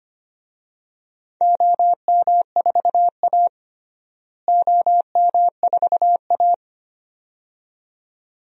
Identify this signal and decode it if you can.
Morse code, heard as a keyed tone, about 25 wpm, 700 Hz. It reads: OM4A OM4A
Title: DXCC-100 Call Signs - 4 Characters - Repeated 2x - 25wpm